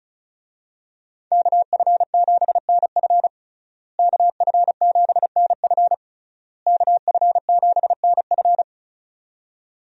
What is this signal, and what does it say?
Morse code, heard as a keyed tone, about 36 wpm, 700 Hz. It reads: KF7NF KF7NF KF7NF
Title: USA Call Signs - 5 Characters - Repeated 3x - 35wpm